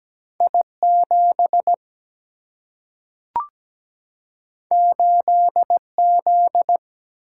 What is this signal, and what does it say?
Morse code, heard as a keyed tone, about 17 wpm, 700 Hz. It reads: I7 8Z
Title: Single-Letter Single-Number 17wpm